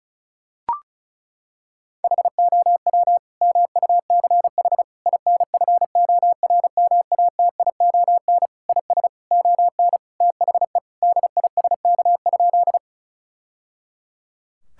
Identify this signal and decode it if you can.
Morse code, heard as a keyed tone, about 36 wpm, 700 Hz. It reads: HOW MUCH INFORMATION IS ON THE DISK?
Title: Sentences from Top 1000 Words - Review Part 6 - 35wpm